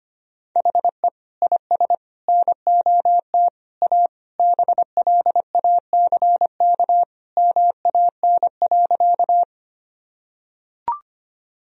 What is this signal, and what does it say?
Morse code, heard as a keyed tone, about 25 wpm, 700 Hz. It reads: HE IS NOT A BLACK MAN.
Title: Sentences from Top 1000 Words - Review Part 2 - 25wpm